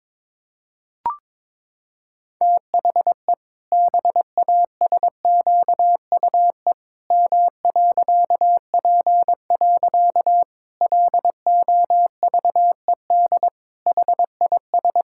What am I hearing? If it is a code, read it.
Morse code, heard as a keyed tone, about 22 wpm, 700 Hz. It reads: THE BASQUE M.P. LOVED HIS
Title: Pangram Challenge v2 - 22wpm